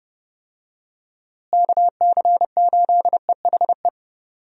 Morse code, heard as a keyed tone, about 30 wpm, 700 Hz. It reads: KC8EHE
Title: USA Call Signs - 6 Characters - Repeated 3x - 30wpm